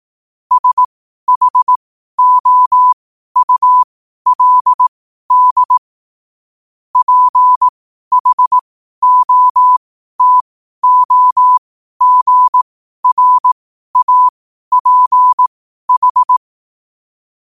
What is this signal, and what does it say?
Morse code, heard as a keyed tone, about 18 wpm, spaced as for 12 wpm, 1000 Hz. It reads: SHOULD PHOTOGRAPH
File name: qotdfiles / 12wpm